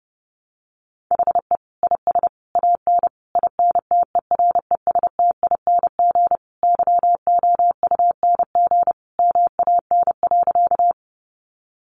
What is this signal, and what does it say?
Morse code, heard as a keyed tone, about 30 wpm, 700 Hz. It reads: HE IS AN INTERESTING YOUNG MAN.
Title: Sentences from Top 1000 Words 30wpm